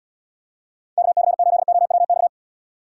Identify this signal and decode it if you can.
Morse code, heard as a keyed tone, about 74 wpm, 700 Hz. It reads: KC8OWX